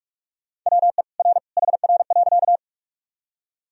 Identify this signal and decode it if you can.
Morse code, heard as a keyed tone, about 47 wpm, 700 Hz. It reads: WE R HR.
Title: Sentences from Top 200 Words and Abbreviations 45wpm